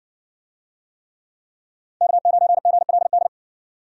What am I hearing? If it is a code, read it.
Morse code, heard as a keyed tone, about 57 wpm, 700 Hz. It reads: K9GLD